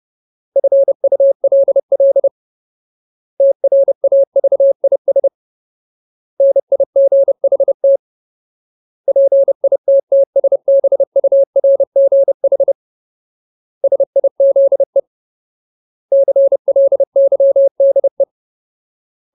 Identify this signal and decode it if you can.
Morse code, heard as a keyed tone, about 30 wpm, 550 Hz. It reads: FULL TRAVIS NIGHT PITTSBURGH SIZE CLYDE
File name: Ultimate Word List / 30WPM Ultimate